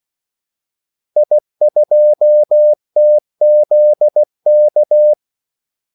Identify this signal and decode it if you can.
Morse code, heard as a keyed tone, about 16 wpm, 600 Hz. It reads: I2TZK
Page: Kaldesignaler 16 wpm